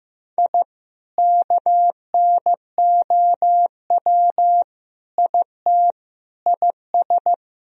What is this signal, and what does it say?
Morse code, heard as a keyed tone, about 15 wpm, 700 Hz. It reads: I KNOW IT IS
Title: Sentences from Top 1000 Words - Review Part 4 - 15wpm